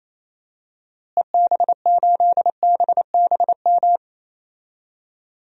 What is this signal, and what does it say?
Morse code, heard as a keyed tone, about 28 wpm, 700 Hz. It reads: EB8BBM
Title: DXCC-100 Call Signs - 6 Characters - Repeated 2x - 28wpm